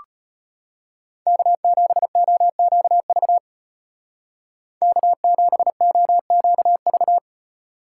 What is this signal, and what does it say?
Morse code, heard as a keyed tone, about 32 wpm, 700 Hz. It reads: K7OQV K7OQV